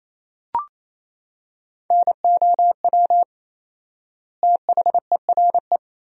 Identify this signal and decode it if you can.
Morse code, heard as a keyed tone, about 28 wpm, 700 Hz. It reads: NOW THERE
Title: 4x Word-Spacing - Sentences from Top 100 Words - 28wpm